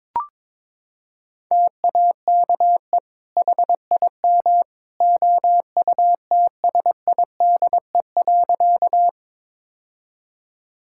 Morse code, heard as a keyed tone, about 22 wpm, 700 Hz. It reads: TAKE HIM OUTSIDE.
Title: Sentences from Top 1000 Words - Review Part 6 - 22wpm